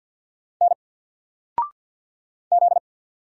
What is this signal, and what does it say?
Morse code, heard as a keyed tone, about 50 wpm, 700 Hz. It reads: N Z
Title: Single Letter-Number Alt 50wpm